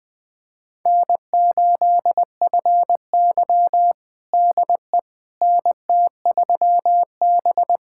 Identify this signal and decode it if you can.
Morse code, heard as a keyed tone, about 20 wpm, 700 Hz. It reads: N8FY DE NT3B
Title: USA Callsign DE Callsign - 4 Characters - 20wpm